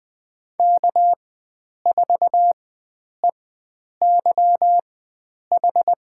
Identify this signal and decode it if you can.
Morse code, heard as a keyed tone, about 20 wpm, spaced as for 9 wpm, 700 Hz. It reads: K4EYH
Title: USA Call Signs - 5 Characters - 4x Character Spacing - 20wpm